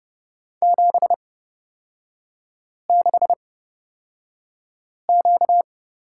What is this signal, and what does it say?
Morse code, heard as a keyed tone, about 30 wpm, spaced as for 5 wpm, 690 Hz. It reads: 76Q
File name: Lesson 38e - Letter q - Three Characters 30wpm